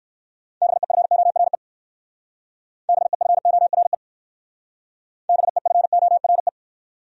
Morse code, heard as a keyed tone, about 57 wpm, 700 Hz. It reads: BEFORE BEFORE BEFORE